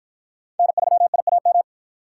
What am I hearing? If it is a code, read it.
Morse code, heard as a keyed tone, about 53 wpm, 700 Hz. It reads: N3IAM